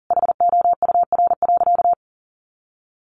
Morse code, heard as a keyed tone, about 40 wpm, 700 Hz. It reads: HOUR.
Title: Sentences from Top 1300 Words 40wpm